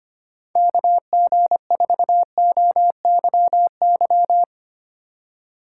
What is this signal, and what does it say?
Morse code, heard as a keyed tone, about 31 wpm, 700 Hz. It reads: KG4OYY